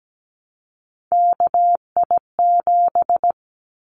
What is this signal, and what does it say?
Morse code, heard as a keyed tone, about 17 wpm, 700 Hz. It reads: KI7
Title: Callsign Prefixes for Top 25 DXCC - Repeated 3x - 17wpm